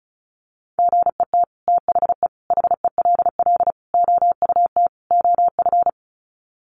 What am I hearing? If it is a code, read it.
Morse code, heard as a keyed tone, about 34 wpm, 700 Hz. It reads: GET THE HELL OUT OF
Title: Sets of 5 Words Encore2 35wpm